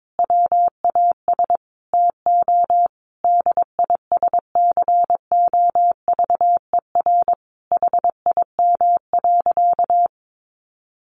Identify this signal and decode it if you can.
Morse code, heard as a keyed tone, about 22 wpm, 700 Hz. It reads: WAS TO DISCOVER HIM.
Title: Sentences from Top 1100 Words 22wpm